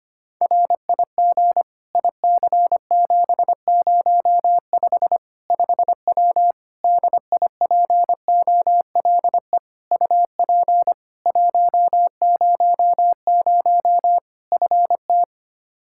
Morse code, heard as a keyed tone, about 25 wpm, 700 Hz. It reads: RIG IC705 5W DIPOLE UP 100 FT
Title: K1USN SST QSO Practice 25wpm